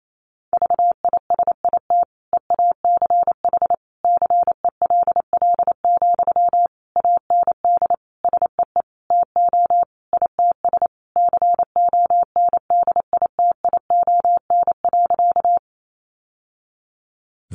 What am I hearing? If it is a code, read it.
Morse code, heard as a keyed tone, about 28 wpm, 700 Hz. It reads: VISIT EACH CELL, AND SEE TO ITS CONDITION.
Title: Sentences from Top 1100 Words 28wpm